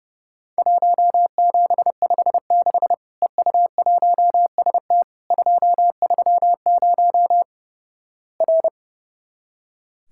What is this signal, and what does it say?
Morse code, heard as a keyed tone, about 32 wpm, 700 Hz. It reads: 1756 EU1ST 230 R
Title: Worked All Europe DX Contest - QTC Exchange - 30wpm